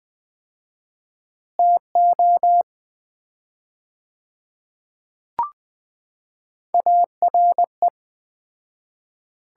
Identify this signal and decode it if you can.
Morse code, heard as a keyed tone, about 20 wpm, 700 Hz. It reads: TO ARE
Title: 4x Word-Spacing - Sets of 4 Words - 20wpm